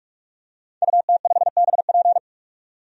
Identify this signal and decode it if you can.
Morse code, heard as a keyed tone, about 46 wpm, 700 Hz. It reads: UT5BP
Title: DXCC-100 Call Signs - 5 Characters - 45wpm